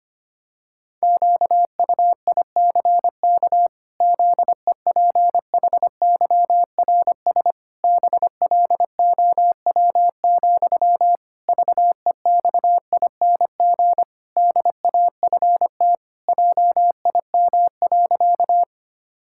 Morse code, heard as a keyed tone, about 25 wpm, 700 Hz. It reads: QUICK ZEPHYRS BLOW, VEXING DAFT JIM.
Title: Pangram Challenge v2 - 25wpm